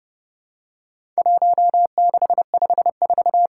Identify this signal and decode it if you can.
Morse code, heard as a keyed tone, about 30 wpm, 700 Hz. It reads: 1654
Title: Worked All Europe DX Contest - QTC Exchange - 30wpm